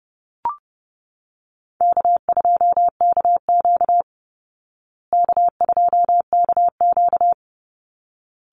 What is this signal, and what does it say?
Morse code, heard as a keyed tone, about 30 wpm, 700 Hz. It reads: K2KQ K2KQ
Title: Easy Call Signs Repeated 2x - 30wpm